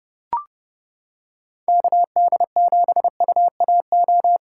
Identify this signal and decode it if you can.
Morse code, heard as a keyed tone, about 32 wpm, 700 Hz. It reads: KD7UAO